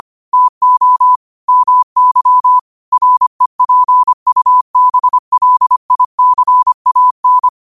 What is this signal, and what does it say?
Morse code, heard as a keyed tone, about 25 wpm, 1000 Hz. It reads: TO MY REPUBLICAN
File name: qotdfiles / 25wpm